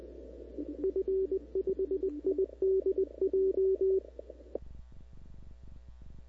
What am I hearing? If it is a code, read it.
Morse code, heard as a keyed tone, about 21 wpm, 380 Hz. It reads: F5IDJ
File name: CW / 2005-k5tr arrl cw dx -17:00